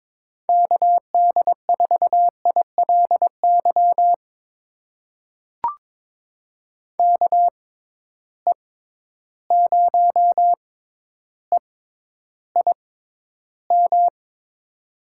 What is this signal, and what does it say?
Morse code, heard as a keyed tone, about 22 wpm, spaced as for 11 wpm, 700 Hz. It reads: KD4ILY KE0EIM
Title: USA Call Signs - 6 Characters - 6x Character Spacing - 22wpm